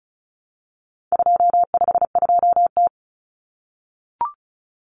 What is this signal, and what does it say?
Morse code, heard as a keyed tone, about 34 wpm, 700 Hz. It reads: 252T
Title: Common Cut Numbers - 4-Digit 35wpm